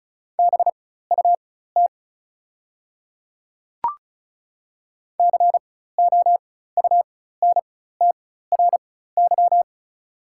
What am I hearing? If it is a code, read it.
Morse code, heard as a keyed tone, about 36 wpm, spaced as for 16 wpm, 700 Hz. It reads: BUT COUNTRY
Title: 4x Character Spacing - Top 300 Words - 35wpm